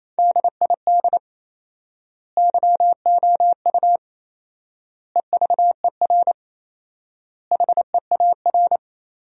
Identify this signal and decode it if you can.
Morse code, heard as a keyed tone, about 27 wpm, 700 Hz. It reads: DID YOU EVER HEAR
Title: 4x Word-Spacing - Sentences from Top 400 Words - 28wpm